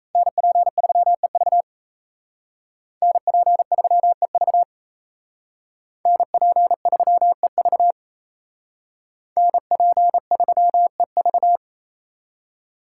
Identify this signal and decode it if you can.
Morse code, heard as a keyed tone, about 35 wpm, 700 Hz. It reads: NP3EV NP3EV NP3EV NP3EV